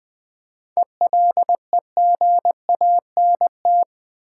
Morse code, heard as a keyed tone, about 20 wpm, 700 Hz. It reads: ELEGANT
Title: 7 Letter Words - First 3 Letters Given - 20wpm